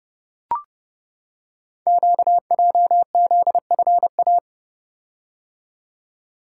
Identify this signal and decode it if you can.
Morse code, heard as a keyed tone, about 29 wpm, 690 Hz. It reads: QJZFA
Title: Code Groups - 5 Letters - 30wpm